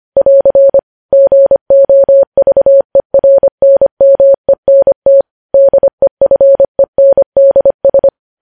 Morse code, heard as a keyed tone, about 25 wpm, 550 Hz. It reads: <AR> GOVERNMENT DEFENDS
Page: news headlines in morse code at 25 wpm